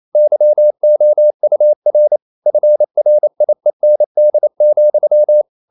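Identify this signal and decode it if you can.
Morse code, heard as a keyed tone, about 28 wpm, 600 Hz. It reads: YOUR FRIEND,
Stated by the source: Morse code practice at 28 wpm, text of Ken Follett's novel Never in English, from CW med Gnister.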